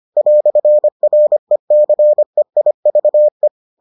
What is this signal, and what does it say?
Morse code, heard as a keyed tone, about 25 wpm, 600 Hz. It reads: "RECEIVE